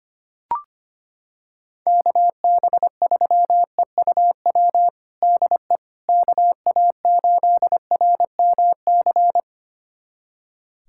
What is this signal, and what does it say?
Morse code, heard as a keyed tone, about 25 wpm, 700 Hz. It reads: KB3EUW DE KA8RMC
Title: USA Callsign DE Callsign - 6 Characters - 25wpm